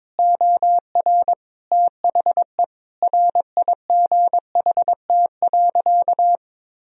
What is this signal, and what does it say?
Morse code, heard as a keyed tone, about 21 wpm, 700 Hz. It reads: OR THE RIGHT.
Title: Sentences from Top 1000 Words - Review Part 4 - 22wpm